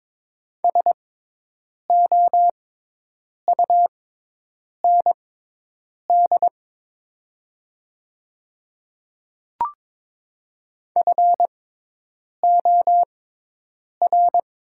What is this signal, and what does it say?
Morse code, heard as a keyed tone, about 22 wpm, spaced as for 8 wpm, 700 Hz. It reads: SOUND FOR
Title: 6x Character Spacing - Top 100 Words - 22wpm